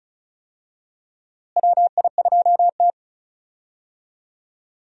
Morse code, heard as a keyed tone, about 36 wpm, 700 Hz. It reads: WI2T
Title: USA Call Signs - 4 Characters - Repeated 2x - 35wpm